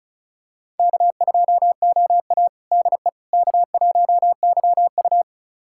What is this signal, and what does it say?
Morse code, heard as a keyed tone, about 36 wpm, 700 Hz. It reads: K2OA DE K1YU
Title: USA Callsign DE Callsign - 4 Characters - 35wpm